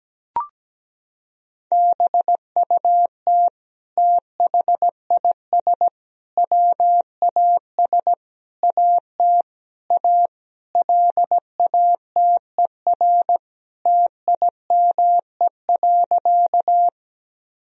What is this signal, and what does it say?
Morse code, heard as a keyed tone, about 17 wpm, 700 Hz. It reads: BUT THIS WAS AT A LATER TIME.
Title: Sentences from Top 1000 Words - Review Part 4 - 17wpm